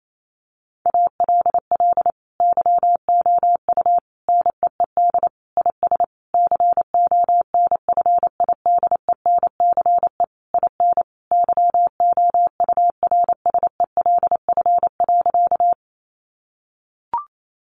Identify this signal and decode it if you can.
Morse code, heard as a keyed tone, about 28 wpm, 700 Hz. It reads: ALL YOU NEED IS CONFIDENCE IN YOURSELF.
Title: Sentences from Top 1500 Words 28wpm